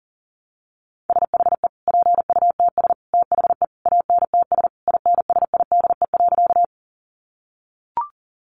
Morse code, heard as a keyed tone, about 40 wpm, 700 Hz. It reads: SHE PUTS THE ANTS INSIDE.
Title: Additional Easy Phrases 40wpm